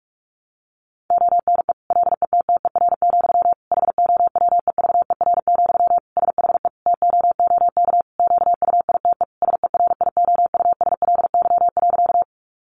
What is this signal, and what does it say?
Morse code, heard as a keyed tone, about 44 wpm, 700 Hz. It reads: ONE LETTER, HOWEVER, SHE TOOK QUITE SERIOUSLY.
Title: Sentences from Top 1200 Words 45wpm